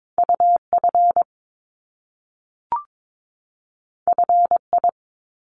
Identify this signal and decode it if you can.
Morse code, heard as a keyed tone, about 22 wpm, 700 Hz. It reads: UF FI
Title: Top Two Letter Combinations in Words 22wpm